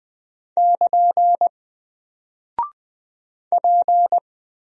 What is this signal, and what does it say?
Morse code, heard as a keyed tone, about 20 wpm, 700 Hz. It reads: <KN> P
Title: Single Character (Letter:Number:Punctuation:Prosign) Alt 20wpm